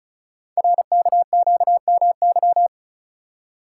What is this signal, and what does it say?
Morse code, heard as a keyed tone, about 36 wpm, 700 Hz. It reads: RKQMY
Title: Code Groups - 5 Letter:Numbers - 35wpm